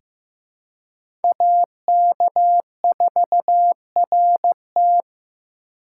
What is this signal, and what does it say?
Morse code, heard as a keyed tone, about 15 wpm, 700 Hz. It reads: AK4RT
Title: USA Call Signs - 5 Characters - 2x Character Spacing - 15wpm